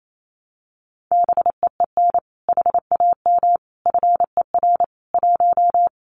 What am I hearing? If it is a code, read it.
Morse code, heard as a keyed tone, about 28 wpm, 700 Hz. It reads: BEEN HAM FER 1
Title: Common QSO Phrases Encore 28wpm